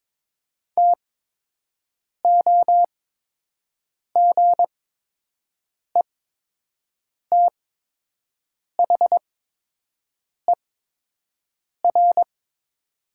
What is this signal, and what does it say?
Morse code, heard as a keyed tone, about 22 wpm, spaced as for 6 wpm, 700 Hz. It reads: TOGETHER
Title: 8x Character Spacing - Top 300 Words - 22wpm